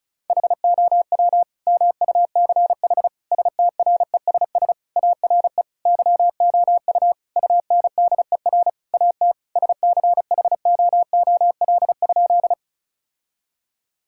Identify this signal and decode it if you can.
Morse code, heard as a keyed tone, about 34 wpm, 700 Hz. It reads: HOW MUCH STRESS ARE YOU UNDER AT SCHOOL?
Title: Sentences from Top 1000 Words - Review Part 6 - 35wpm